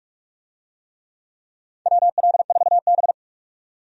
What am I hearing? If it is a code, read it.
Morse code, heard as a keyed tone, about 46 wpm, 700 Hz. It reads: WL4B
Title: USA Call Signs - 4 Characters - 45wpm